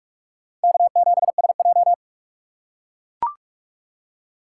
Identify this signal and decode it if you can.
Morse code, heard as a keyed tone, about 42 wpm, 700 Hz. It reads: K7SJ
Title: USA Call Signs - 4 Characters - 4x Character Spacing - 45wpm